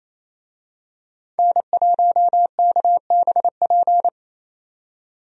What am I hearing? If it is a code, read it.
Morse code, heard as a keyed tone, about 28 wpm, 700 Hz. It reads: N1KBP
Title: USA Call Signs - 5 Characters - 28wpm